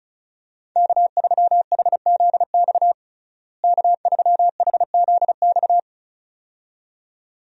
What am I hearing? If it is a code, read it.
Morse code, heard as a keyed tone, about 34 wpm, 700 Hz. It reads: K3HZX K3HZX
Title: USA Call Signs - 5 Characters - Repeated 2x - 35wpm